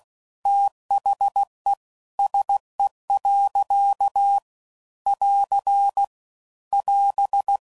Morse code, heard as a keyed tone, about 16 wpm, 800 Hz. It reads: THE SE. <AR> <AS>
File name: CW Code Groups @15WPM / Weather Sets / Weather 4 17